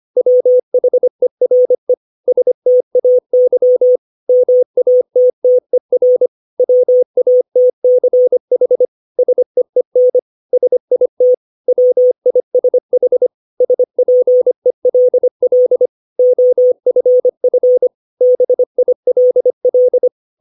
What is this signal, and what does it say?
Morse code, heard as a keyed tone, about 25 wpm, 500 Hz. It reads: WHERE STAY MATTER WATCH SEEN SIT WISH SPELL OFF BILL
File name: 632 words / 632 Words 25WPM